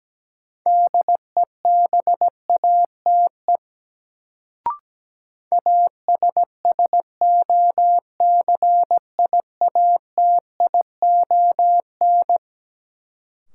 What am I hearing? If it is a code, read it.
Morse code, heard as a keyed tone, about 17 wpm, 700 Hz. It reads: DEBATE ASSOCIATION
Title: Sentences from Top 1100 Words - New Word Drill - 17wpm